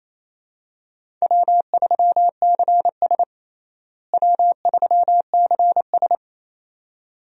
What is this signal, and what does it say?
Morse code, heard as a keyed tone, about 28 wpm, 700 Hz. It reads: W3CS W3CS
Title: USA Call Signs - 4 Characters - Repeated 2x - 28wpm